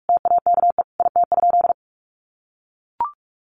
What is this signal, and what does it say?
Morse code, heard as a keyed tone, about 44 wpm, 700 Hz. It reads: TAKE IT?
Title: Sentences from Top 300 Words 45wpm